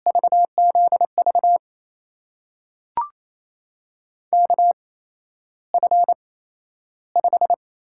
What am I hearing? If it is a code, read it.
Morse code, heard as a keyed tone, about 28 wpm, spaced as for 12 wpm, 700 Hz. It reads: VZV KF5
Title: USA Call Signs - 6 Characters - 8x Character Spacing - 28wpm